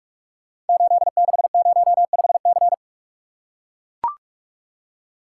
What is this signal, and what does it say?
Morse code, heard as a keyed tone, about 44 wpm, 700 Hz. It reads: 8605C
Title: Code Groups - 5 Letter:Numbers - 45wpm